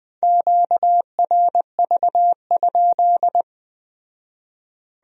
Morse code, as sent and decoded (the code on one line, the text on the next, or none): --.- .-. ...- ..--..
QRV?